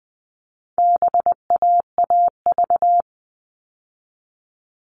Morse code, heard as a keyed tone, about 20 wpm, 700 Hz. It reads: BAAV
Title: All Cut Numbers - 4-Digit 20wpm